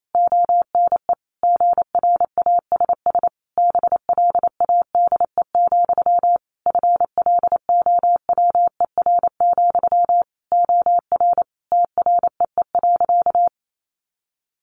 Morse code, heard as a keyed tone, about 28 wpm, 700 Hz. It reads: ONE GRASS BLADE, FLOWER, OR TREE.
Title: A Sound Of Thunder 28wpm